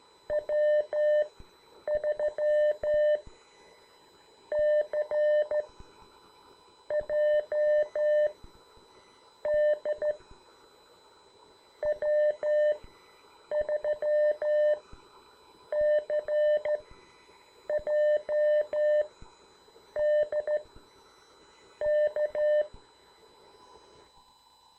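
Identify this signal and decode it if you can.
Morse code, heard as a keyed tone, about 13 wpm, spaced as for 6 wpm, 590 Hz. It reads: W3CJD W3CJD K
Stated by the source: CW class lesson from The Martian Net with instructor N2KPS.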